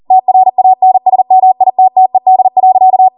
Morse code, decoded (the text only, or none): TRANSMITTED.